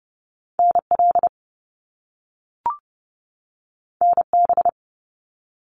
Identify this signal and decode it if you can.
Morse code, heard as a keyed tone, about 30 wpm, 700 Hz. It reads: NL NB
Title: Canadian Provinces and Territories 30wpm